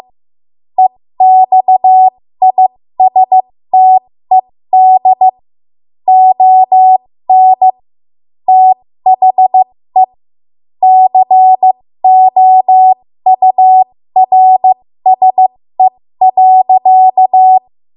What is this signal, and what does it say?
Morse code, heard as a keyed tone, about 15 wpm, spaced as for 13 wpm, 750 Hz. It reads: EXISTED ON THE COURSE.
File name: Morse / 13WPM